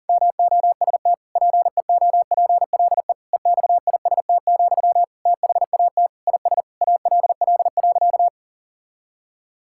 Morse code, heard as a keyed tone, about 40 wpm, 700 Hz. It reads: MOST PEOPLE EXIST, THAT IS ALL.
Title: Sentences from Top 1000 Words - Review Part 5 - 40wpm